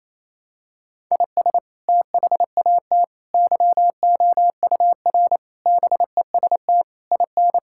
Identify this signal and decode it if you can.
Morse code, heard as a keyed tone, about 27 wpm, 700 Hz. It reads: IS THAT YOUR BEST IN
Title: Sentences from Top 1000 Words - Review Part 5 - 28wpm